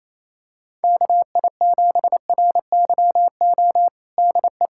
Morse code, heard as a keyed tone, about 28 wpm, 700 Hz. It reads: KI7RYO DE
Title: USA Callsign DE Callsign - 6 Characters - 28wpm